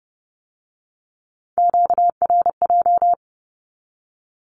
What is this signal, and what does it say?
Morse code, heard as a keyed tone, about 32 wpm, 700 Hz. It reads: QRJ
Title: Q-Codes with Spoken Meaning - 30wpm